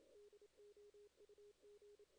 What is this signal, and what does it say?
Morse code, heard as a keyed tone, about 27 wpm, 440 Hz. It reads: DOUG
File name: 2016 JAN CW NAQP / 2016 Jan naqp k5ot - 0-11